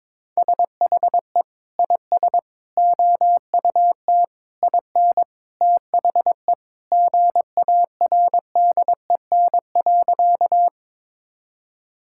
Morse code, heard as a keyed tone, about 22 wpm, 700 Hz. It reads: SHE IS OUT IN THE GARDEN.
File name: practice / 22wpm / Sentences with 3-Word Phrases - 22wpm